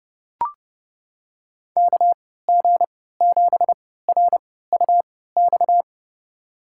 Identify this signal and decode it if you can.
Morse code, heard as a keyed tone, about 32 wpm, spaced as for 17 wpm, 700 Hz. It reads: KG7RUX
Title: USA Call Signs - 6 Characters - 3x Character Spacing - 30wpm